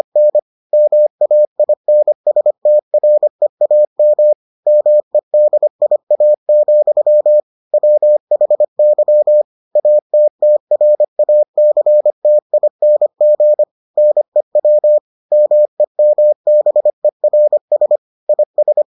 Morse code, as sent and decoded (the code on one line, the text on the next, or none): -. / -- .- .. -. ... - .-. .- -- / -- . -.. .. .- --..-- / .-- .... -.-- / .- - - .-. .- -.-. - .. -. --. / -. .-- / -- . -- -... .-. ... / .. ...
N MAINSTREAM MEDIA, WHY ATTRACTING NEW MEMBERS IS